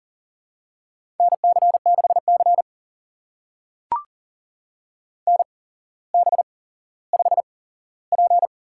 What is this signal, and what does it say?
Morse code, heard as a keyed tone, about 39 wpm, spaced as for 17 wpm, 700 Hz. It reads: NC6C NB5P
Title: USA Call Signs - 4 Characters - 8x Character Spacing - 40wpm